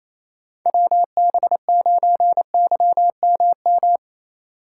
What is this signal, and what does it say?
Morse code, heard as a keyed tone, about 28 wpm, 700 Hz. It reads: WB9YMM
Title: USA Call Signs - 6 Characters - 2x Character Spacing - 28wpm